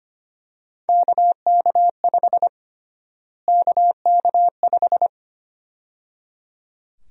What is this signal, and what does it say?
Morse code, heard as a keyed tone, about 25 wpm, 700 Hz. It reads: KK5 KK5
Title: USA Call Sign Prefixes - Repeated 2x - 25wpm